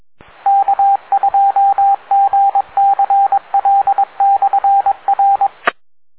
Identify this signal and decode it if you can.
Morse code, heard as a keyed tone, about 22 wpm, 790 Hz, with radio noise. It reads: K2GCL/R